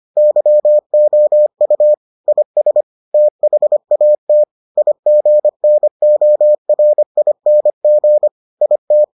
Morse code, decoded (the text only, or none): YOU IS THAT IGNORING IT